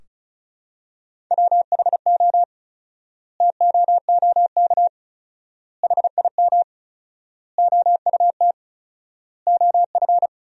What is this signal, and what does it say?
Morse code, heard as a keyed tone, about 34 wpm, 700 Hz. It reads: WHO TOOK HIM OUT OF
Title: 4x Word-Spacing - Sentences from Top 300 Words - 35wpm